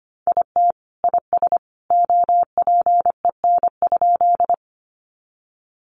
Morse code, heard as a keyed tone, about 25 wpm, 700 Hz. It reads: IT IS OPEN?
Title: Sentences from Top 300 Words 25wpm